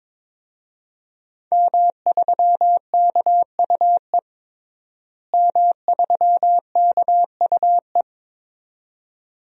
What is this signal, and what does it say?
Morse code, heard as a keyed tone, about 22 wpm, 700 Hz. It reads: M3KUE M3KUE
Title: DXCC-100 Call Signs - 5 Characters - Repeated 2x - 22wpm